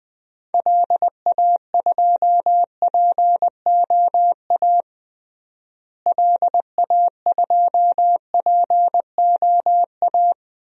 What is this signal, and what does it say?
Morse code, heard as a keyed tone, about 20 wpm, 700 Hz. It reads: LA2POA LA2POA
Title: DXCC-100 Call Signs - 6 Characters - Repeated 2x - 20wpm